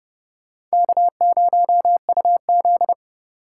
Morse code, heard as a keyed tone, about 36 wpm, 700 Hz. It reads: K0UZ
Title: Speed Racing - USA 1x2 Call Signs 30wpm